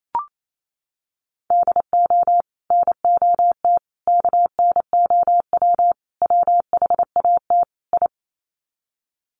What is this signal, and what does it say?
Morse code, heard as a keyed tone, about 28 wpm, 700 Hz. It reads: DO NOT KNOW WHAT I
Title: Sets of 5 Words 28wpm